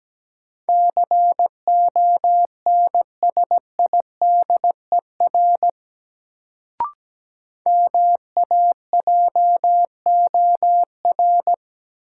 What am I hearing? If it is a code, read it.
Morse code, heard as a keyed tone, about 17 wpm, 700 Hz. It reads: CONSIDER MAJOR
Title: Sentences from Top 600 Words - New Word Drill - 17wpm